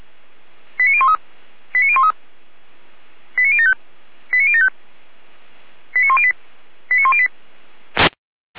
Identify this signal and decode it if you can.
Five-tone selective call (ZVEI), heard as quick runs of five tones, tones of about 70 ms. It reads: 89013 89013 89075 89075 89108 89108